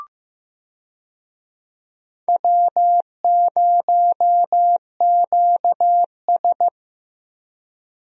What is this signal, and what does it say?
Morse code, heard as a keyed tone, about 15 wpm, 700 Hz. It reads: W0QS
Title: USA Call Signs - 4 Characters - Repeated 3x - 15wpm